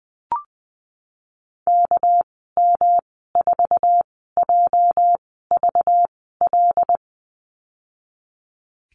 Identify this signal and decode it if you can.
Morse code, heard as a keyed tone, about 19 wpm, spaced as for 14 wpm, 700 Hz. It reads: KM4JVL
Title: 245 Lesson 39g - Letter z - Call Signs 20wpm